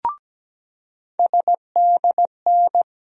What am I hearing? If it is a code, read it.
Morse code, heard as a keyed tone, about 17 wpm, 700 Hz. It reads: SDN
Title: USA Call Sign Postfixes - 3 Letters 17wpm